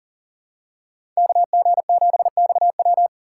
Morse code, heard as a keyed tone, about 40 wpm, 700 Hz. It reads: KG7XW
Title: USA Call Signs - 5 Characters - 6x Character Spacing - 40wpm